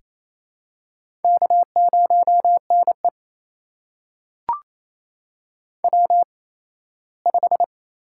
Morse code, heard as a keyed tone, about 28 wpm, spaced as for 17 wpm, 700 Hz. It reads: K0NE W5
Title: USA Call Signs - 8x Character Spacing - 28wpm